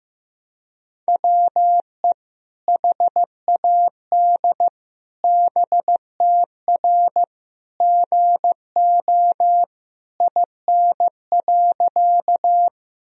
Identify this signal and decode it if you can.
Morse code, heard as a keyed tone, about 15 wpm, 700 Hz. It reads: WE HAD BTR GO IN.
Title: Sentences from Top 400 Words and Abbreviations 15wpm